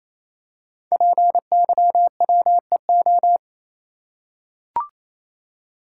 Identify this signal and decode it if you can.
Morse code, heard as a keyed tone, about 29 wpm, 700 Hz. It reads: PYWEO
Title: Code Groups - 5 Letters - 28wpm